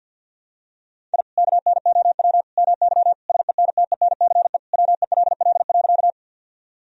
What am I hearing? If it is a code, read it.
Morse code, heard as a keyed tone, about 49 wpm, 700 Hz. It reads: I KNOW MY SENTENCE WELL.
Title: Sentences from Top 1000 Words - Review Part 1 - 50wpm